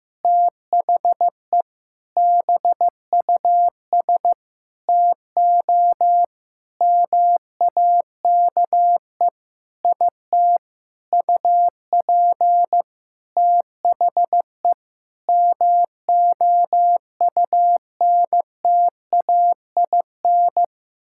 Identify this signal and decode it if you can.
Morse code, heard as a keyed tone, about 15 wpm, 700 Hz. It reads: THE BUS TO MAKE IT UP THE MOUNTAIN
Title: Sentences from Top 1000 Words - Review Part 6 - 15wpm